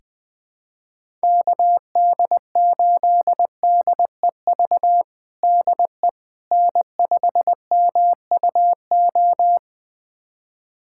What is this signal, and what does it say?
Morse code, heard as a keyed tone, about 20 wpm, 700 Hz. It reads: KD8DEV DE N5MUO
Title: USA Callsign DE Callsign 20wpm